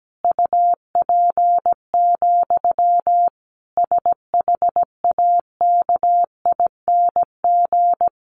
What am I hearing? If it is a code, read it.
Morse code, heard as a keyed tone, about 17 wpm, 700 Hz. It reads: UP, SHAKING
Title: A Sound Of Thunder 17wpm